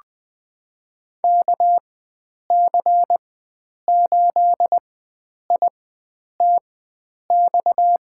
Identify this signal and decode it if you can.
Morse code, heard as a keyed tone, about 20 wpm, spaced as for 9 wpm, 700 Hz. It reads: KC8ITX
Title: USA Call Signs - 6 Characters - 4x Character Spacing - 20wpm